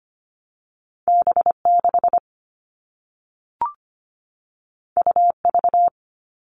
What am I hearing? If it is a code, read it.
Morse code, heard as a keyed tone, about 25 wpm, 700 Hz. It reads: B6 UV
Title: All Cut Numbers - 2-Digit 25wpm